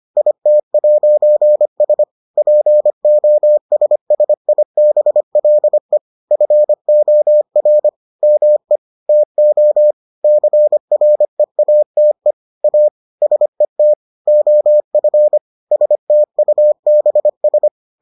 Morse code, heard as a keyed tone, about 25 wpm, 600 Hz. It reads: IT'S POSSIBLE FOR ME TO CREATE A SET OF STUBS